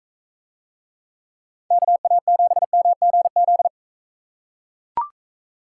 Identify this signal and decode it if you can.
Morse code, heard as a keyed tone, about 43 wpm, 700 Hz. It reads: KA7MGZ